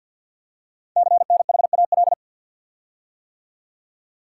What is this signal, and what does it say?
Morse code, heard as a keyed tone, about 48 wpm, 700 Hz. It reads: CNHAL
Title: Code Groups - 5 Letters - 50wpm